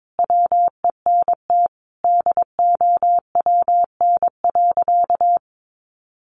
Morse code, heard as a keyed tone, about 22 wpm, 700 Hz. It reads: WENT DOWN.
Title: Sentences from Top 1100 Words 22wpm